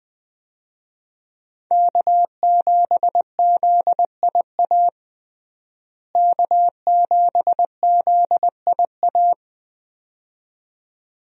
Morse code, heard as a keyed tone, about 20 wpm, 700 Hz. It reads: K7ZIA K7ZIA
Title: USA Call Signs - 5 Characters - Repeated 2x - 20wpm